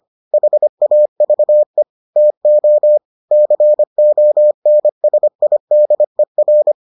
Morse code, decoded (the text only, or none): HAVE TO CONSIDER